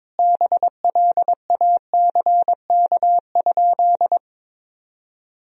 Morse code, heard as a keyed tone, about 22 wpm, 700 Hz. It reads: BLACK?
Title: Sentences from Top 400 Words and Abbreviations 22wpm